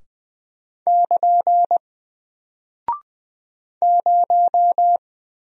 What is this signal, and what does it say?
Morse code, heard as a keyed tone, about 20 wpm, 700 Hz. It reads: <KN> 0
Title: Single Character (Letter:Number:Punctuation:Prosign) Alt 20wpm